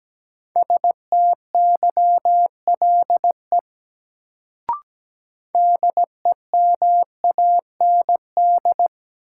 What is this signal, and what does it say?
Morse code, heard as a keyed tone, about 17 wpm, 700 Hz. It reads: STYLE DEMAND
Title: Sentences from Top 800 Words - New Word Drill - 17wpm